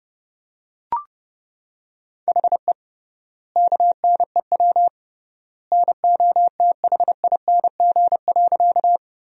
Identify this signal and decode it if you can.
Morse code, heard as a keyed tone, about 30 wpm, 700 Hz. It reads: HE KNEW NOTHING.
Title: 3x Word-Spacing - Sentences from Top 500 Words - 30wpm